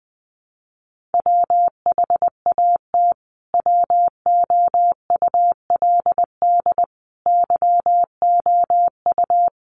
Morse code, heard as a keyed tone, about 20 wpm, 700 Hz. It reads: WHAT WOULD YOU
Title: Sentences from Top 1200 Words 20wpm